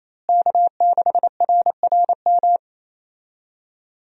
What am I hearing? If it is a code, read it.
Morse code, heard as a keyed tone, about 28 wpm, 700 Hz. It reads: K6RRM
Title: USA Call Signs - 5 Characters - Repeated 2x - 28wpm